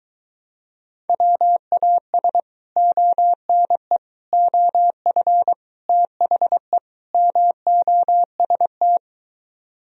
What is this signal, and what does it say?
Morse code, heard as a keyed tone, about 23 wpm, 700 Hz. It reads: WAS ONE OF THE MOST